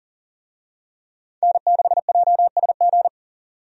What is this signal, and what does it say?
Morse code, heard as a keyed tone, about 40 wpm, 700 Hz. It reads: N6JSG
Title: USA Call Signs - 5 Characters - 40wpm